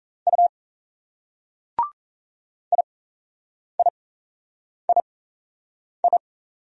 Morse code, heard as a keyed tone, about 34 wpm, spaced as for 8 wpm, 700 Hz. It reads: U IIII
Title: Speed-Racing Single Letters 28wpm